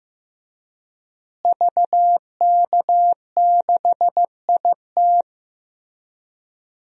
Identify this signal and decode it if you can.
Morse code, heard as a keyed tone, about 15 wpm, 700 Hz. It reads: VK6IT
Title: DXCC-100 Call Signs - 5 Characters - Repeated 3x - 15wpm